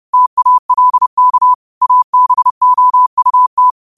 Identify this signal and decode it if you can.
Morse code, heard as a keyed tone, about 30 wpm, 1000 Hz. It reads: TALK ABOUT
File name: qotdfiles / 30wpm